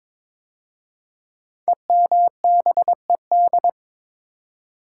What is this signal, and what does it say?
Morse code, heard as a keyed tone, about 22 wpm, 700 Hz. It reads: EMBED
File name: 5 Letter Words - First 3 Letters Given - 22wpm